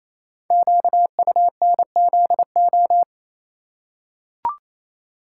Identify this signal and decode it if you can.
Morse code, heard as a keyed tone, about 27 wpm, 700 Hz. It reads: QUNZO
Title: Code Groups - 5 Letters - 28wpm